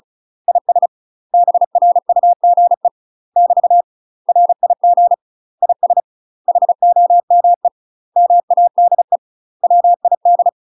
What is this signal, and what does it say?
Morse code, heard as a keyed tone, about 36 wpm, 700 Hz. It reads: IS BRUGE <BT> RIG IS HOME MADE WID
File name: UK MASTER QSOs TRAINING / 35WPM / UK MASTER QSOs 0024